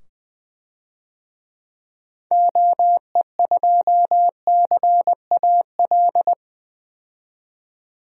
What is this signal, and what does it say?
Morse code, heard as a keyed tone, about 21 wpm, 700 Hz. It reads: OE2CAL
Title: DXCC-100 Call Signs - 6 Characters - 20wpm